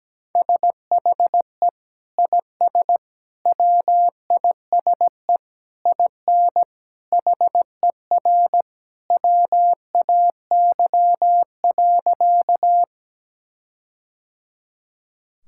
Morse code, heard as a keyed tone, about 17 wpm, 700 Hz. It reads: SHE IS WISE IN HER WAY.
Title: Sentences from Top 1600 Words 17wpm